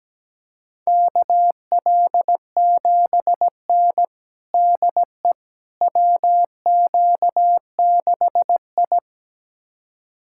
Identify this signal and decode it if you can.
Morse code, heard as a keyed tone, about 17 wpm, 700 Hz. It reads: KL7N DE WQ6I
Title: USA Callsign DE Callsign - 4 Characters - 17wpm